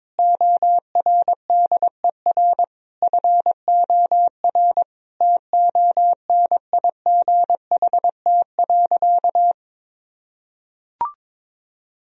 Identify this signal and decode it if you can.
Morse code, heard as a keyed tone, about 22 wpm, 700 Hz. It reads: ORDER FOR TONIGHT.
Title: Sentences from Top 1000 Words - Review Part 3 - 22wpm